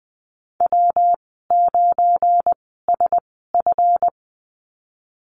Morse code, heard as a keyed tone, about 20 wpm, spaced as for 15 wpm, 700 Hz. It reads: W9SF
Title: 186 Lesson 30g - Letter b - Call Signs 20wpm